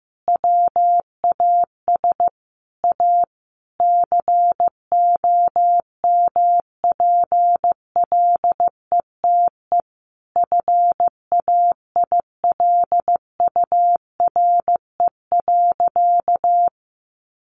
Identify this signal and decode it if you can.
Morse code, heard as a keyed tone, about 15 wpm, 700 Hz. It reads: WAS A COMPLETE FAILURE.
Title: Sentences from Top 1400 Words 15wpm